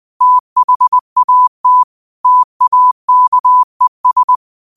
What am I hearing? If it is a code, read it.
Morse code, heard as a keyed tone, about 20 wpm, 1000 Hz. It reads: THAT TAKES
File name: qotdfiles / 20wpm